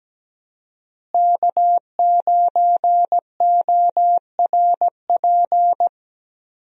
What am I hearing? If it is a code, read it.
Morse code, heard as a keyed tone, about 17 wpm, 700 Hz. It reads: K9ORP
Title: USA Call Signs - 5 Characters - Repeated 2x - 17wpm